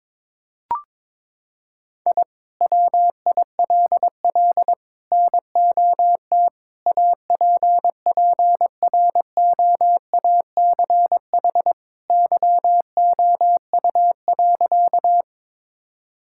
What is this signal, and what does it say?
Morse code, heard as a keyed tone, about 22 wpm, 700 Hz. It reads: I WILL NOT APPROACH YOU.
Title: Sentences from Top 1000 Words - Review Part 6 - 22wpm